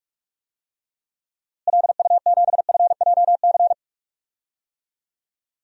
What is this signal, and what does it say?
Morse code, heard as a keyed tone, about 41 wpm, 700 Hz. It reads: LU7FJC
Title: DXCC-100 Call Signs - 6 Characters - Repeated 3x - 45wpm